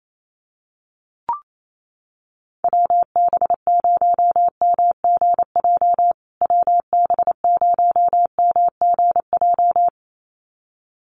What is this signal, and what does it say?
Morse code, heard as a keyed tone, about 28 wpm, 700 Hz. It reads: WB0MGJ WB0MGJ
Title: USA Call Signs Repeated 2x 28wpm